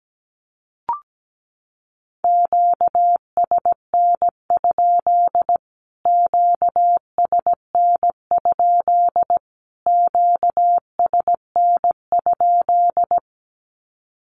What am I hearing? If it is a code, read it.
Morse code, heard as a keyed tone, about 17 wpm, 700 Hz. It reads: QSN? QSN? QSN?
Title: Q-Codes Repeated 3x 17wpm